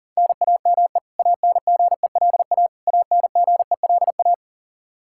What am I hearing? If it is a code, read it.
Morse code, heard as a keyed tone, about 40 wpm, 700 Hz. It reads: NAME ANGELA ANGELA
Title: Practice QSO - Exchange 1 - Beginning - 40wpm